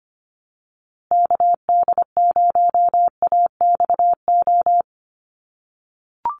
Morse code, heard as a keyed tone, about 25 wpm, 700 Hz. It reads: KD0AXO
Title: USA Call Signs Repeated 2x 25wpm